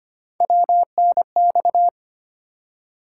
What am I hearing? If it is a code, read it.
Morse code, heard as a keyed tone, about 25 wpm, 700 Hz. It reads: WNX
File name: Three Letter-Number Combinations - 25wpm